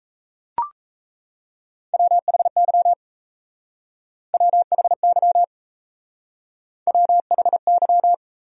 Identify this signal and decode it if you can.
Morse code, heard as a keyed tone, about 37 wpm, 700 Hz. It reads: WHY WHY WHY